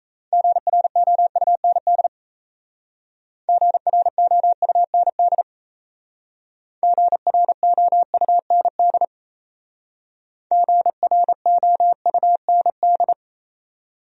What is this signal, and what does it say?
Morse code, heard as a keyed tone, about 37 wpm, 700 Hz. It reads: GROUND GROUND GROUND GROUND